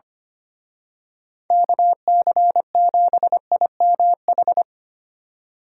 Morse code, heard as a keyed tone, about 28 wpm, 700 Hz. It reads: KC7IMH
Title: Speed Racing - USA Call Signs 25wpm